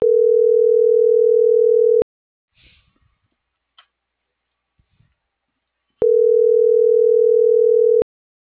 Ringback tone (North America), 440 + 480 Hz.